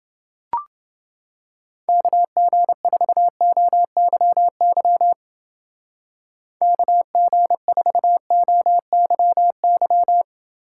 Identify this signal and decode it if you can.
Morse code, heard as a keyed tone, about 28 wpm, 700 Hz. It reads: KG4OYY KG4OYY